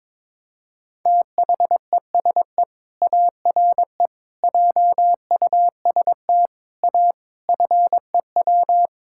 Morse code, heard as a keyed tone, about 22 wpm, 700 Hz. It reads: THESE ARE JUST A FEW
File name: practice / 22wpm / Sets of 7 Words - 22wpm